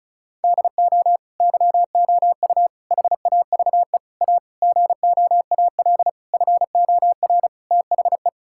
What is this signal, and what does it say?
Morse code, heard as a keyed tone, about 36 wpm, 700 Hz. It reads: DO YOU HAVE A GOAL FOR THE
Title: Sentences from Top 1000 Words - Review Part 5 - 35wpm